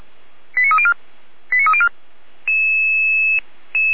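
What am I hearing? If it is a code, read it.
Five-tone selective call (ZVEI), heard as quick runs of five tones, tones of about 70 ms. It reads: 89304 89304
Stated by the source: ZVEI five-tone call received by SDR, archived by Feuerwehr Orlamünde (fire brigade).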